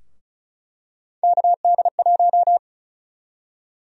Morse code, heard as a keyed tone, about 36 wpm, 700 Hz. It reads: KD1
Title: USA Call Sign Prefixes - 35wpm